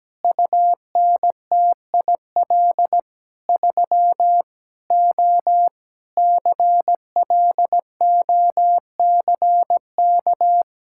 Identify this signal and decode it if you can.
Morse code, heard as a keyed tone, about 17 wpm, 700 Hz. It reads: UNTIL 3 O CLOCK
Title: Pangram Challenge v2 - 17wpm